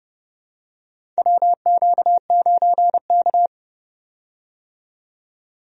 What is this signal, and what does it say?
Morse code, heard as a keyed tone, about 30 wpm, 700 Hz. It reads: WQ9K
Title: USA Call Signs - 4 Characters - 30wpm